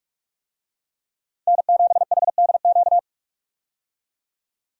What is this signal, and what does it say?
Morse code, heard as a keyed tone, about 46 wpm, 700 Hz. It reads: N7HDQ
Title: USA Call Signs - 5 Characters - 45wpm